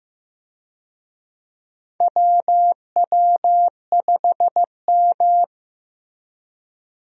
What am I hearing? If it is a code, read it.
Morse code, heard as a keyed tone, about 15 wpm, 700 Hz. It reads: WW5M
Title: USA Call Signs - 4 Characters - Repeated 2x - 15wpm